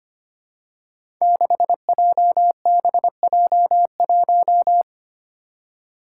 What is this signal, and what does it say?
Morse code, heard as a keyed tone, about 25 wpm, 700 Hz. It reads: 6JBJ1
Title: Five Challenging Letter-Number Combinations 25wpm